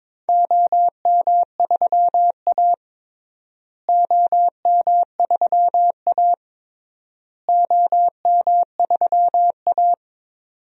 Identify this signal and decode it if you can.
Morse code, heard as a keyed tone, about 22 wpm, 700 Hz. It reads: OM3A OM3A OM3A
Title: DXCC-100 Call Signs - 4 Characters - Repeated 3x - 22wpm